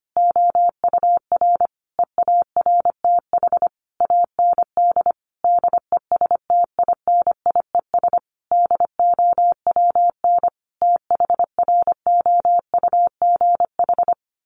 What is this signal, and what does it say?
Morse code, heard as a keyed tone, about 25 wpm, 700 Hz. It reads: OUR EARTH AND DESTINIES DOWN THROUGH
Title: A Sound Of Thunder 25wpm